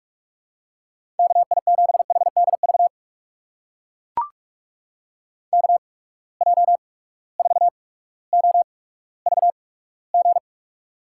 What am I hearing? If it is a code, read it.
Morse code, heard as a keyed tone, about 43 wpm, spaced as for 19 wpm, 700 Hz. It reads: KI7HDV KJ4OVG
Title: USA Call Signs - 6 Characters - 8x Character Spacing - 45wpm